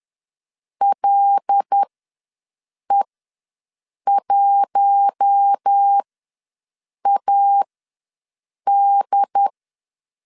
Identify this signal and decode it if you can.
Morse code, heard as a keyed tone, about 10 wpm, spaced as for 6 wpm, 800 Hz. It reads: LE1AD
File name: LEZIONI / 5 LBVC / 5 L ws 10wpm 3636